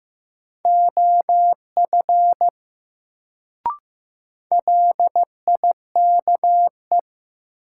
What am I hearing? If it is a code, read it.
Morse code, heard as a keyed tone, about 15 wpm, 700 Hz. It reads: OF LIKE
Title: Sentences from Top 100 Words - New Word Drill - 15wpm